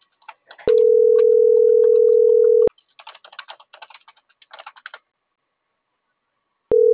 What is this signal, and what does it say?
Ringback tone (North America), 440 + 480 Hz.